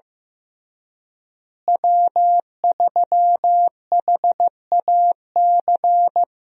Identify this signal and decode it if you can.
Morse code, heard as a keyed tone, about 15 wpm, 700 Hz. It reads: W3HAC
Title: USA Call Signs - 5 Characters - Repeated 3x - 15wpm